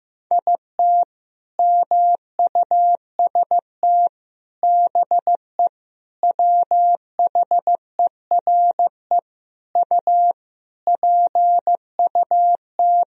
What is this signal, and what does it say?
Morse code, heard as a keyed tone, about 15 wpm, 700 Hz. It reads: IT MUST BE WHERE U PUT
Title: Sentences from Top 200 Words and Abbreviations 15wpm